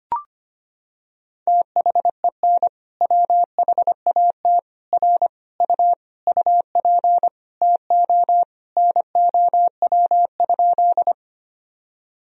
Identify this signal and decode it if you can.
Morse code, heard as a keyed tone, about 25 wpm, 700 Hz. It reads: THEN WHAT R U UP TO NOW?
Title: Sentences from Top 100 Words and Abbreviations 25wpm